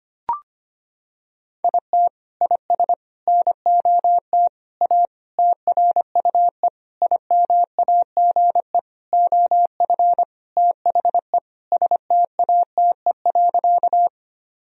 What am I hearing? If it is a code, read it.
Morse code, heard as a keyed tone, about 25 wpm, 700 Hz. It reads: IT IS NOT A TRUE IMAGE OF THE STATE.
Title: Sentences from Top 1000 Words - Review Part 4 - 25wpm